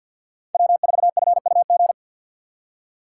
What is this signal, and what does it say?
Morse code, heard as a keyed tone, about 53 wpm, 700 Hz. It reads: W4FUG
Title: USA Call Signs - 5 Characters - 3x Character Spacing - 50wpm